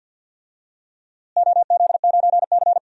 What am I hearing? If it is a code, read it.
Morse code, heard as a keyed tone, about 48 wpm, 700 Hz. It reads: OZ9C
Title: DXCC-100 Call Signs - 4 Characters - Repeated 2x - 50wpm